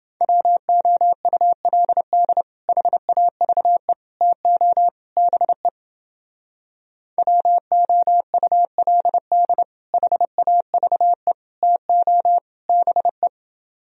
Morse code, heard as a keyed tone, about 30 wpm, 700 Hz. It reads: WOULD HAVE TO BE WOULD HAVE TO BE